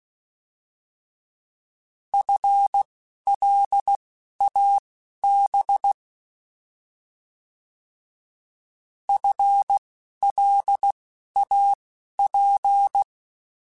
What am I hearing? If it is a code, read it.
Morse code, heard as a keyed tone, about 16 wpm, spaced as for 11 wpm, 800 Hz. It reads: FLAB FLAP
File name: Alphabetical / Alphabetical mp3s / Fs @10